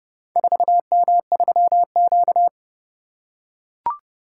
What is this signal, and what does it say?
Morse code, heard as a keyed tone, about 29 wpm, 700 Hz. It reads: 4M3Q